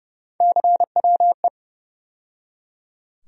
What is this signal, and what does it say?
Morse code, heard as a keyed tone, about 31 wpm, 700 Hz. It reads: CWE
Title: USA Call Sign Postfixes - 3 Letters 30wpm